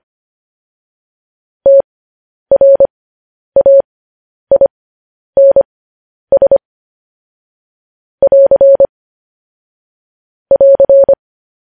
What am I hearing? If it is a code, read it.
Morse code, heard as a keyed tone, about 25 wpm, spaced as for 10 wpm, 550 Hz. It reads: TRAINS <AR> <AR>
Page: News Headlines in Morse Code at 10 WPM